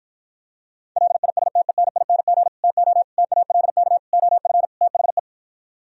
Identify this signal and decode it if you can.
Morse code, heard as a keyed tone, about 52 wpm, 700 Hz. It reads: LISTENING TO TALK OF THE